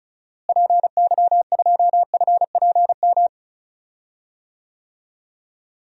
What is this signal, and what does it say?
Morse code, heard as a keyed tone, about 36 wpm, 700 Hz. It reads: PY2FPM
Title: DXCC-100 Call Signs - 6 Characters - Repeated 2x - 35wpm